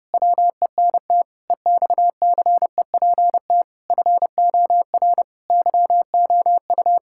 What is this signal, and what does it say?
Morse code, heard as a keyed tone, about 28 wpm, 700 Hz. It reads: WENT EXCEPT FOR YOU
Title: Sentences from Top 1600 Words 30wpm